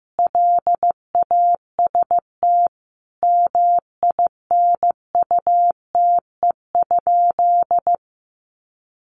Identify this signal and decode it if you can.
Morse code, heard as a keyed tone, about 15 wpm, 700 Hz. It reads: LAST MINUTE?
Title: Sentences from Top 1100 Words 15wpm